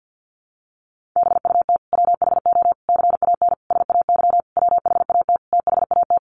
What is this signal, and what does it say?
Morse code, heard as a keyed tone, about 50 wpm, 700 Hz. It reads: BUT WHO CAN SAY WHAT THAT
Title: Sentences from Top 1500 Words 50wpm